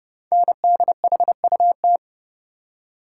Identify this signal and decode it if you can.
Morse code, heard as a keyed tone, about 29 wpm, 700 Hz. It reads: NDHUT
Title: Code Groups - 5 Letters - 30wpm